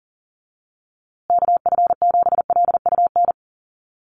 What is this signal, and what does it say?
Morse code, heard as a keyed tone, about 40 wpm, 700 Hz. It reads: KF7LUN
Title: USA Call Signs Repeated 3x 40wpm